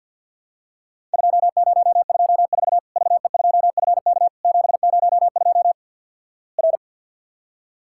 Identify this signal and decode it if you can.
Morse code, heard as a keyed tone, about 52 wpm, 690 Hz. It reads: 2024 VE2FK 702 R
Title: Worked All Europe DX Contest - QTC Exchange with Repeats - 50wpm